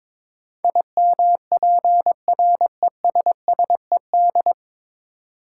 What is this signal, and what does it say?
Morse code, heard as a keyed tone, about 21 wpm, 700 Hz. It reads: IMPRESSED
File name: 9 Letter Words - First 3 Letters Given - 22wpm